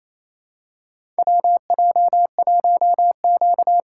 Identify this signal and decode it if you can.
Morse code, heard as a keyed tone, about 27 wpm, 700 Hz. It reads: WJ1Q
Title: USA Call Signs - 2x Character Spacing - 28wpm